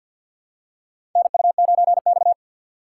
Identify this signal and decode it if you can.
Morse code, heard as a keyed tone, about 48 wpm, 700 Hz. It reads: NU9X
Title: USA Call Signs - 4x Character Spacing - 50wpm